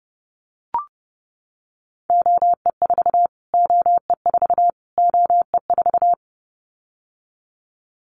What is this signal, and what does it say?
Morse code, heard as a keyed tone, about 30 wpm, 700 Hz. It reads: OE4 OE4 OE4
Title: Callsign Prefixes for Top 25 DXCC - Repeated 3x - 30wpm